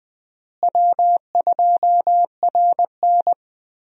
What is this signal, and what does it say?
Morse code, heard as a keyed tone, about 20 wpm, 700 Hz. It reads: W2RN
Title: USA Call Signs - 4 Characters - Repeated 3x - 20wpm